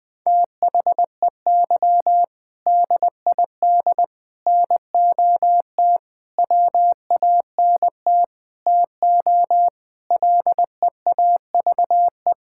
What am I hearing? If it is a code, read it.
Morse code, heard as a keyed tone, about 19 wpm, 700 Hz. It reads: THEY DID NOT WANT TO LEAVE
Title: Sentences with 5-Word Phrases - 20wpm